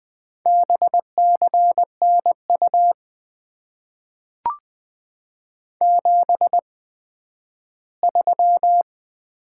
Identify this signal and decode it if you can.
Morse code, heard as a keyed tone, about 20 wpm, spaced as for 12 wpm, 700 Hz. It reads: BCNU 73
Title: Top 90 QSO Elements - 8x Character Spacing - 20wpm